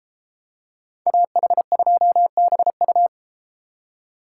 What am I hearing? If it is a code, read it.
Morse code, heard as a keyed tone, about 31 wpm, 700 Hz. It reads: AH2BU